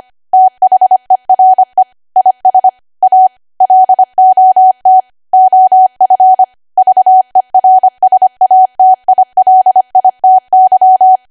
Morse code, heard as a keyed tone, about 25 wpm, 750 Hz. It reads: THERE IS A LOT OF VERSATILITY
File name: cwa / W1AW-Level3 / 25WPM